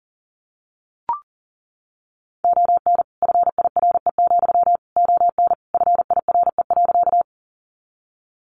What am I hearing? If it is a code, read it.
Morse code, heard as a keyed tone, about 40 wpm, 700 Hz. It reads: ON FIRE, ON FIRE.
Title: Sentences from Top 400 Words 40wpm